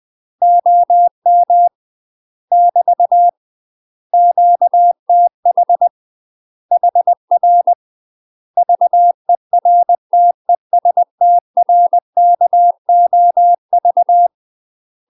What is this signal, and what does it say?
Morse code, heard as a keyed tone, about 20 wpm, 700 Hz. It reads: OM <BT> QTH HR VERTESTRKOV
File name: UK MASTER QSOs TRAINING / 20WPM / UK MASTER QSOs 0058